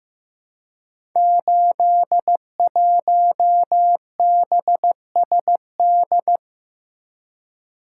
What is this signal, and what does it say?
Morse code, heard as a keyed tone, about 15 wpm, 700 Hz. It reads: 81BSD
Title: Code Groups - 5 Letter:Numbers - 15wpm